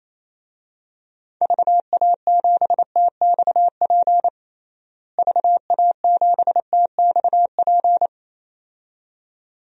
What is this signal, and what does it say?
Morse code, heard as a keyed tone, about 28 wpm, 700 Hz. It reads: VA7TXP VA7TXP
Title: DXCC-100 Call Signs - 6 Characters - Repeated 2x - 28wpm